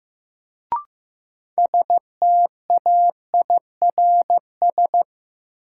Morse code, heard as a keyed tone, about 15 wpm, 700 Hz. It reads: STAIRS
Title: Sets of 5 Words Encore - New Word Drill - 15wpm